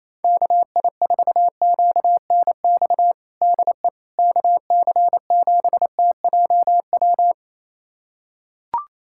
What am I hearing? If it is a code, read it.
Morse code, heard as a keyed tone, about 28 wpm, 700 Hz. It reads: KI4QNX DE KC7TJW
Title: USA Callsign DE Callsign - 6 Characters - 28wpm